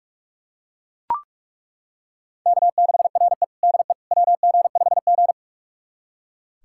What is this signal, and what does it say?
Morse code, heard as a keyed tone, about 42 wpm, 700 Hz. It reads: K6RE DE WG5G
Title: USA Callsign DE Callsign - 4 Characters - 45wpm